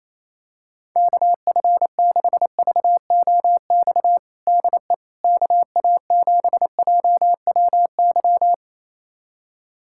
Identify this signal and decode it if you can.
Morse code, heard as a keyed tone, about 28 wpm, 700 Hz. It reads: KF6VOX DE KA7JWY
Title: USA Callsign DE Callsign - 6 Characters - 28wpm